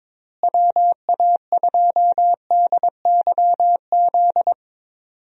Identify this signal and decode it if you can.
Morse code, heard as a keyed tone, about 21 wpm, 700 Hz. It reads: WA2DYZ
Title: USA Call Signs - 6 Characters - Repeated 3x - 22wpm